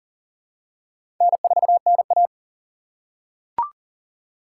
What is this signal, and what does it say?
Morse code, heard as a keyed tone, about 40 wpm, 700 Hz. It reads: N4NA
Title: USA Call Signs - 4 Characters - 6x Character Spacing - 40wpm